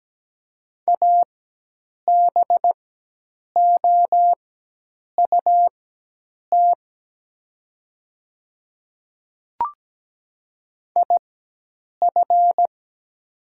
Morse code, heard as a keyed tone, about 17 wpm, spaced as for 8 wpm, 700 Hz. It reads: ABOUT IF
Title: 4x Character Spacing - Top 100 Words - 17wpm